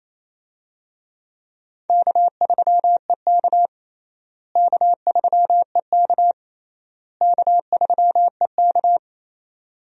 Morse code, heard as a keyed tone, about 28 wpm, 700 Hz. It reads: K3EK K3EK K3EK
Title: USA Call Signs - 4 Characters - Repeated 3x - 28wpm